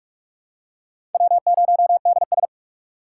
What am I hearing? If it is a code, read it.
Morse code, heard as a keyed tone, about 46 wpm, 700 Hz. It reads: W0DS
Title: USA Call Signs - 4 Characters - 3x Character Spacing - 45wpm